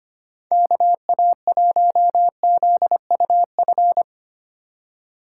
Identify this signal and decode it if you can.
Morse code, heard as a keyed tone, about 25 wpm, 700 Hz. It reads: KA1ZUF
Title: USA Call Signs - 6 Characters - 2x Character Spacing - 25wpm